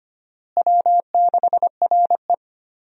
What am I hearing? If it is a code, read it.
Morse code, heard as a keyed tone, about 25 wpm, 700 Hz. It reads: W6RE
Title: USA Call Signs - 4 Characters - Repeated 3x - 25wpm